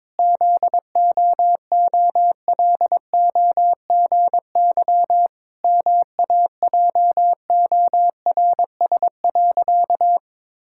Morse code, heard as a keyed tone, about 21 wpm, 700 Hz. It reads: ZOOLOGY MAJORS.
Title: Pangram Challenge v2 - 22wpm